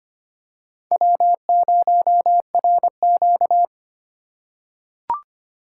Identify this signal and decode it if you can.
Morse code, heard as a keyed tone, about 25 wpm, 700 Hz. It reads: W0RQ
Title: USA Call Signs - 4 Characters - 4x Character Spacing - 25wpm